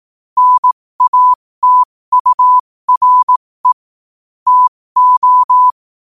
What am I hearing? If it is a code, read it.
Morse code, heard as a keyed tone, about 18 wpm, spaced as for 15 wpm, 1000 Hz. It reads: NATURE TO
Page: New quotes every day in morse code at 15 Words per minute.